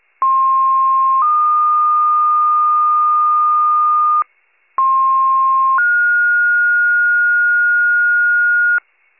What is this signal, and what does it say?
Two-tone page, heard as a one-second tone, then a three-second tone, about 1060 then 1220 Hz, about 1060 then 1430 Hz, with noise. It is heard from 0.2 to 8.8 s.